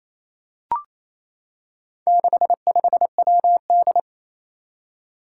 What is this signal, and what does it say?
Morse code, heard as a keyed tone, about 27 wpm, 700 Hz. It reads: 65WD
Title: Four Challenging Letter-Number Combinations 28wpm